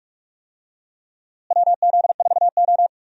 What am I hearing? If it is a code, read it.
Morse code, heard as a keyed tone, about 42 wpm, 700 Hz. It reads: WZ4O
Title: USA Call Signs - 4 Characters - Repeated 2x - 45wpm